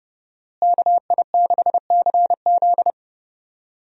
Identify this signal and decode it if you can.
Morse code, heard as a keyed tone, about 30 wpm, 700 Hz. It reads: KI6CZ
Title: USA Call Signs - 5 Characters - Repeated 2x - 30wpm